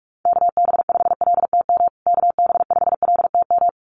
Morse code, heard as a keyed tone, about 44 wpm, 700 Hz. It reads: KB5LTM KB5LTM
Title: USA Call Signs Repeated 2x 45wpm